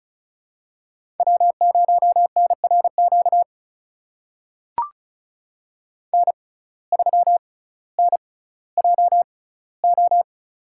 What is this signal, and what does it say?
Morse code, heard as a keyed tone, about 36 wpm, spaced as for 18 wpm, 700 Hz. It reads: W0NRQ N3NJO
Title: USA Call Signs - 5 Characters - 6x Character Spacing - 35wpm